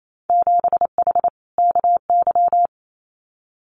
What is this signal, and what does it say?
Morse code, heard as a keyed tone, about 28 wpm, 700 Hz. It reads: 7H KY
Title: Winter Field Day Exchanges Encore 28wpm